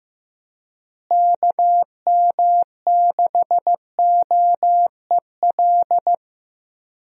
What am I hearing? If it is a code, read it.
Morse code, heard as a keyed tone, about 15 wpm, 700 Hz. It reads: KM6OEL
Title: USA Call Signs - 6 Characters - Repeated 3x - 15wpm